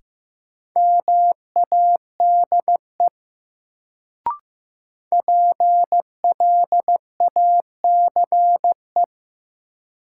Morse code, heard as a keyed tone, about 15 wpm, 700 Hz. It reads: MADE PLACE
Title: Sentences from Top 200 Words - New Word Drill - 15wpm